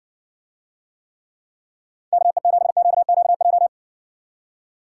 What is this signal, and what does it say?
Morse code, heard as a keyed tone, about 64 wpm, 700 Hz. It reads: KE7QQJ